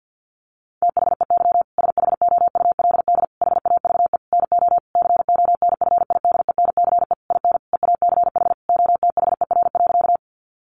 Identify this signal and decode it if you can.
Morse code, heard as a keyed tone, about 50 wpm, 700 Hz. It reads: THEY SHOULD HAVE NO CONFIDENCE IN EACH OTHER.
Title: Sentences from Top 1500 Words 50wpm